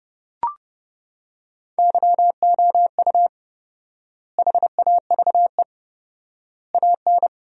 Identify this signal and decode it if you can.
Morse code, heard as a keyed tone, about 32 wpm, 700 Hz. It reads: YOU HAVE AN
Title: 4x Word-Spacing - Sentences from Top 400 Words - 30wpm